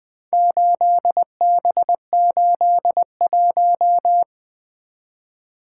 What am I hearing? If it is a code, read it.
Morse code, heard as a keyed tone, about 20 wpm, 700 Hz. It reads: 8B81
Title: Four Letter-Number Combinations - 20wpm